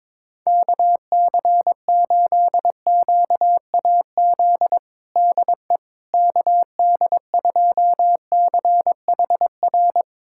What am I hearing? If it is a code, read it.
Morse code, heard as a keyed tone, about 21 wpm, 700 Hz. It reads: KC8QAZ DE KD2CHR
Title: USA Callsign DE Callsign - 6 Characters - 22wpm